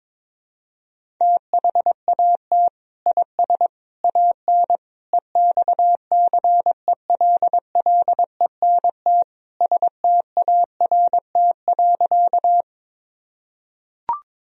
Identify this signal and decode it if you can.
Morse code, heard as a keyed tone, about 22 wpm, 700 Hz. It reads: THAT IS AN EXCELLENT START.
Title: Sentences from Top 1000 Words - Review Part 1 - 22wpm